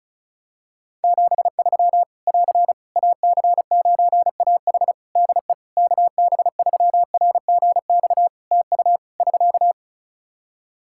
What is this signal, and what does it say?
Morse code, heard as a keyed tone, about 34 wpm, 700 Hz. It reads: 73 <AR> AC9AH DE KB3RGX TU <SK>
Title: Practice QSO - Exchange 4 - Ending - 35wpm